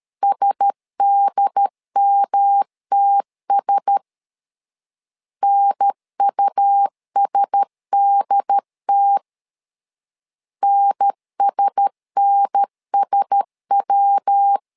Morse code, heard as a keyed tone, about 13 wpm, 800 Hz. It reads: SDMTS NUSDT NSNSW
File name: LEZIONI / 4 Q1S 12wpm 1802